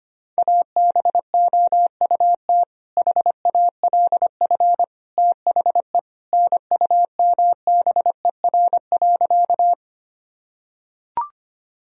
Morse code, heard as a keyed tone, about 25 wpm, 700 Hz. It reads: ABOUT HALF THE NUMBER.
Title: Sentences from Top 1000 Words - Review Part 1 - 25wpm